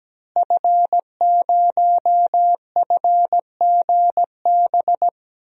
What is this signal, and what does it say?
Morse code, heard as a keyed tone, about 17 wpm, 700 Hz. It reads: F0FGB
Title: DXCC-100 Call Signs - 5 Characters - Repeated 3x - 17wpm